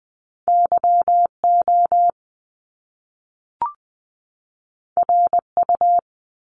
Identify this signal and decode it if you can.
Morse code, heard as a keyed tone, about 20 wpm, 700 Hz. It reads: YO RU
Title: Top Two Letter Combinations in Words 20wpm